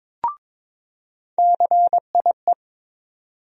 Morse code, heard as a keyed tone, about 22 wpm, 700 Hz. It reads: CIE
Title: USA Call Sign Postfixes - 3 Letters 22wpm